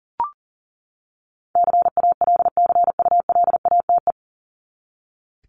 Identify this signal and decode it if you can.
Morse code, heard as a keyed tone, about 40 wpm, 700 Hz. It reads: CALCULATE
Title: Top 1751-2000 Words 40wpm